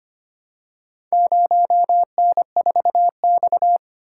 Morse code, heard as a keyed tone, about 25 wpm, 700 Hz. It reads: 0N4X
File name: practice / 25wpm / Four Letter-Number Combinations - 25wpm